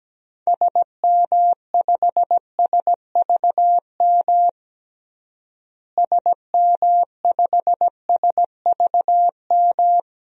Morse code, heard as a keyed tone, about 17 wpm, 700 Hz. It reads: SM5SVM SM5SVM
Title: DXCC-100 Call Signs - 6 Characters - Repeated 2x - 17wpm